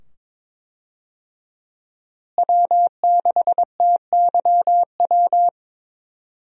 Morse code, heard as a keyed tone, about 22 wpm, 700 Hz. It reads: W6TYW
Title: USA Call Signs - 5 Characters - Repeated 2x - 22wpm